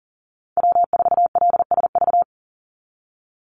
Morse code, heard as a keyed tone, about 40 wpm, 700 Hz. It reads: W4LSV
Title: Call Signs Repeated 3x - 40wpm